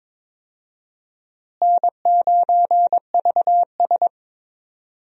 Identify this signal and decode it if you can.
Morse code, heard as a keyed tone, about 21 wpm, 700 Hz. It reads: N9VS
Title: USA Call Signs - 4 Characters - Repeated 2x - 22wpm